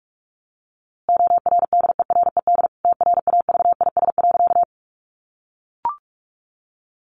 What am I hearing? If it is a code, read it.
Morse code, heard as a keyed tone, about 44 wpm, 700 Hz. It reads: ORDERED TRAVIS.
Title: A Sound Of Thunder 45wpm